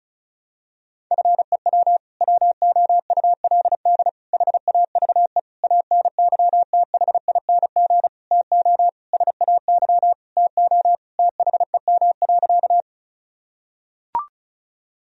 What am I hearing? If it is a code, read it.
Morse code, heard as a keyed tone, about 34 wpm, 700 Hz. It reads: FEW WOULD HAVE ANYTHING TO SAY TO THEM.
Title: Sentences from Top 1000 Words - Review Part 1 - 35wpm